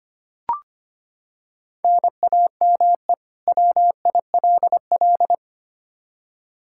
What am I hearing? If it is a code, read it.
Morse code, heard as a keyed tone, about 25 wpm, 700 Hz. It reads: NAME WILL
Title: K1USN SST QSO Practice 25wpm